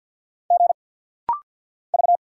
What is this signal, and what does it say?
Morse code, heard as a keyed tone, about 50 wpm, 700 Hz. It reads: G V
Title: Single Letter-Number Alt - Rapid-Fire 50wpm